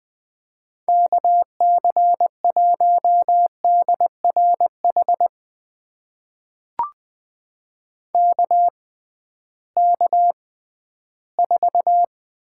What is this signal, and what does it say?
Morse code, heard as a keyed tone, about 20 wpm, spaced as for 13 wpm, 700 Hz. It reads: KC1DRH KK4
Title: USA Call Signs - 6x Character Spacing - 20wpm